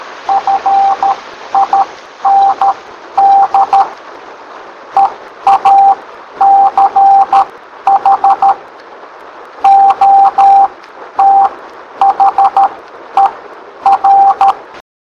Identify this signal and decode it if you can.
Morse code, heard as a keyed tone, about 13 wpm, spaced as for 10 wpm, 790 Hz. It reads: FIND EACH OTHER